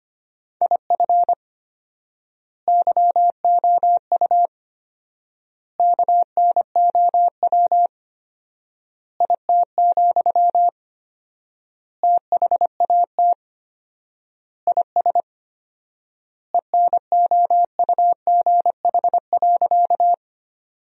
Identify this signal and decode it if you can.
Morse code, heard as a keyed tone, about 25 wpm, 700 Hz. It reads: IF YOU KNOW IT, THAT IS ENOUGH.
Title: 4x Word-Spacing - Sentences from Top 400 Words - 25wpm